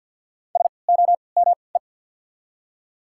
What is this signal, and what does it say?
Morse code, heard as a keyed tone, about 48 wpm, spaced as for 28 wpm, 700 Hz. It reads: SOME
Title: 3x Character Spacing - Top 100 Words - 50wpm